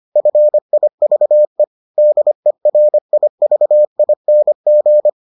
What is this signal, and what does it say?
Morse code, heard as a keyed tone, about 25 wpm, 600 Hz. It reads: FIVE DERIVING